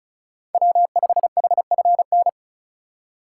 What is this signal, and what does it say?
Morse code, heard as a keyed tone, about 36 wpm, 700 Hz. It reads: W5HFN
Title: USA Call Signs - 2x Character Spacing - 35wpm